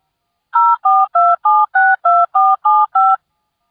DTMF, heard as telephone touch tones, tones of about 220 ms. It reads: 042762475